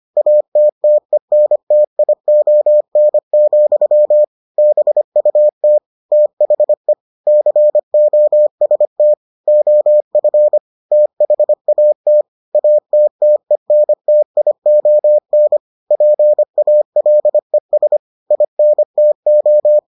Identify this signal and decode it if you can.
Morse code, heard as a keyed tone, about 25 wpm, 600 Hz. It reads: ATTENTION, BUT THE COST OF THAT ATTENTION PALES INTO